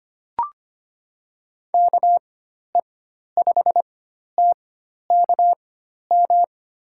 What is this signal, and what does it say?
Morse code, heard as a keyed tone, about 25 wpm, spaced as for 12 wpm, 700 Hz. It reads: KE5TKM
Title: USA Call Signs - 6 Characters - 4x Character Spacing - 25wpm